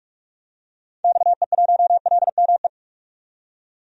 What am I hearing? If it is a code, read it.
Morse code, heard as a keyed tone, about 46 wpm, 700 Hz. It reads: XE1LME
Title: DXCC-100 Call Signs - 6 Characters - 45wpm